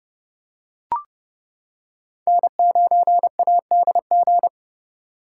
Morse code, heard as a keyed tone, about 30 wpm, 700 Hz. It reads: N9ADG
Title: K1USN SST QSO Practice - 30wpm